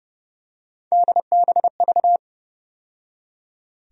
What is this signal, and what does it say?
Morse code, heard as a keyed tone, about 30 wpm, 700 Hz. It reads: DBV
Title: USA Call Sign Postfixes - 3 Letters 30wpm